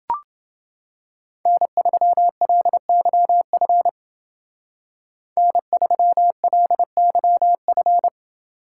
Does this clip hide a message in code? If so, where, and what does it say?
Yes: Morse code from 1.5 to 8.1 s, heard as a keyed tone, about 29 wpm, 700 Hz. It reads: N3LYF N3LYF